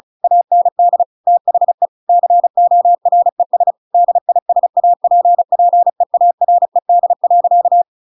Morse code, heard as a keyed tone, about 34 wpm, 700 Hz. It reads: AND THE CORES DISAPPEARED.
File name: Pinocchio / Pinocchio ch7 35 35 ik1hge